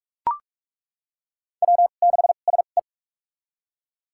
Morse code, heard as a keyed tone, about 41 wpm, spaced as for 31 wpm, 700 Hz. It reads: W6SE
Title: USA Call Signs - 4 Characters - 2x Character Spacing - 45wpm